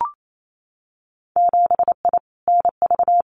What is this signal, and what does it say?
Morse code, heard as a keyed tone, about 28 wpm, 690 Hz. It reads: 7I NV
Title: Winter Field Day Exchanges Encore 28wpm